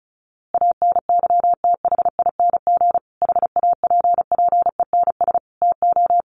Morse code, heard as a keyed tone, about 34 wpm, 700 Hz. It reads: ANYTHING HAPPENS TO
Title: A Sound Of Thunder 35wpm